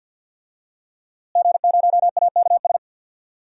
Morse code, heard as a keyed tone, about 51 wpm, 700 Hz. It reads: G0AKS
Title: DXCC-100 Call Signs - 5 Characters - Repeated 2x - 50wpm